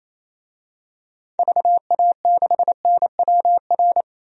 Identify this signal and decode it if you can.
Morse code, heard as a keyed tone, about 28 wpm, 700 Hz. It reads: VA6NWR
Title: DXCC-100 Call Signs - 6 Characters - 28wpm